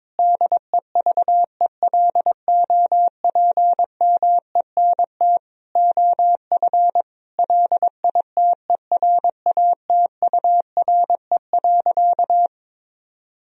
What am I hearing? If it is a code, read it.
Morse code, heard as a keyed tone, about 22 wpm, 700 Hz. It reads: DEVELOPMENT OF LITERATURE.
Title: Sentences from Top 1000 Words - Review Part 4 - 22wpm